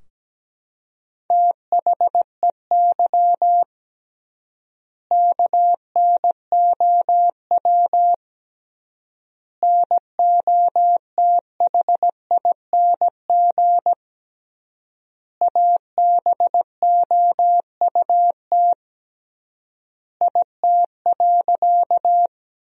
Morse code, heard as a keyed tone, about 17 wpm, 700 Hz. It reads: THEY KNOW NOTHING ABOUT IT.
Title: 3x Word-Spacing - Sentences from Top 500 Words - 17wpm